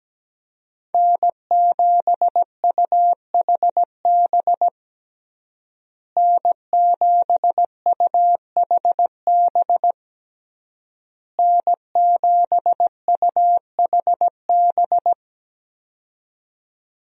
Morse code, heard as a keyed tone, about 17 wpm, 700 Hz. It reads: N7UHB N7UHB N7UHB
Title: USA Call Signs - 5 Characters - Repeated 3x - 17wpm